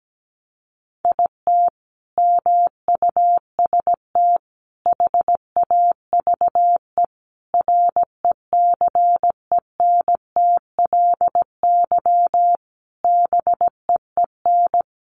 Morse code, heard as a keyed tone, about 17 wpm, 700 Hz. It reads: IT MUST HAVE RECENTLY BEEN
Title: Sentences from Top 1200 Words 17wpm